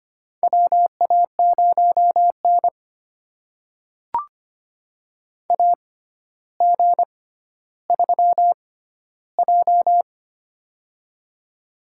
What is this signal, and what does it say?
Morse code, heard as a keyed tone, about 25 wpm, spaced as for 13 wpm, 700 Hz. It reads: WA0N AG3J
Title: USA Call Signs - 4 Characters - 6x Character Spacing - 25wpm